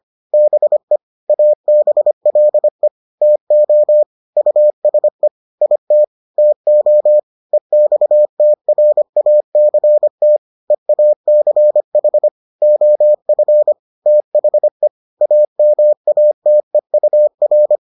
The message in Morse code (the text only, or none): BE ABLE TO USE IT TO EXTRACT EACH OF THE AMATEUR